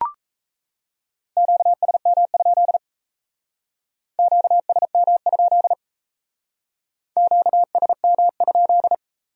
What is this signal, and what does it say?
Morse code, heard as a keyed tone, about 35 wpm, 700 Hz. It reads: QSM? QSM? QSM?